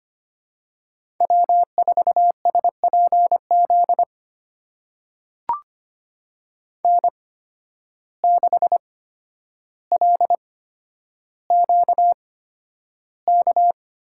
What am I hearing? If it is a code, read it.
Morse code, heard as a keyed tone, about 25 wpm, spaced as for 11 wpm, 700 Hz. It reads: W4SPZ N6LQK
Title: USA Call Signs - 5 Characters - 8x Character Spacing - 25wpm